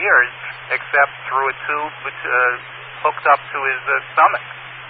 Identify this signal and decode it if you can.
Emergency Alert System (SAME) tones, heard as harsh buzzing bursts, with speech. It is heard from 0.7 to 4.1 s.